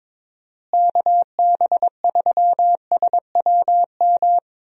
Morse code, heard as a keyed tone, about 21 wpm, 700 Hz. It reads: KB3SWM
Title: USA Call Signs - 6 Characters - Repeated 3x - 22wpm